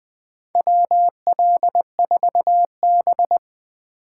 Morse code, heard as a keyed tone, about 20 wpm, 700 Hz. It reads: WL4B
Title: USA Call Signs - 4 Characters - 8x Character Spacing - 20wpm